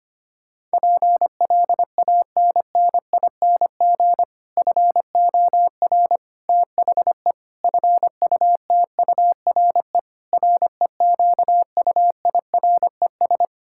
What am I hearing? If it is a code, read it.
Morse code, heard as a keyed tone, about 25 wpm, 700 Hz. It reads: PLANNING FOR THE FUTURE REQUIRES
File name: practice / 25wpm / Difficult Sentences with Hints - 25wpm